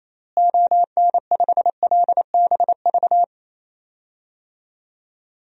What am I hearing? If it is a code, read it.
Morse code, heard as a keyed tone, about 28 wpm, 700 Hz. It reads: ON5LBV
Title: DXCC-100 Call Signs - 6 Characters - Repeated 2x - 28wpm